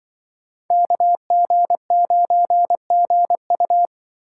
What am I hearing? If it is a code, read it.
Morse code, heard as a keyed tone, about 24 wpm, 700 Hz. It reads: KG9GU
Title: Speed Racing - USA 2x2 Call Signs 20wpm